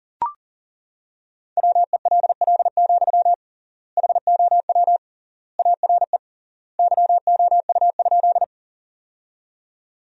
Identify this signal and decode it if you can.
Morse code, heard as a keyed tone, about 37 wpm, 700 Hz. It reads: WELL, HOW ARE YOU?
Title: 3x Word-Spacing - Sentences from Top 200 Words - 40wpm